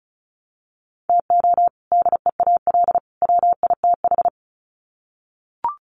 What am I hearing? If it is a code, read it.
Morse code, heard as a keyed tone, about 34 wpm, 700 Hz. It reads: TO DEAL WITH
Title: Sets of 3 Words 35wpm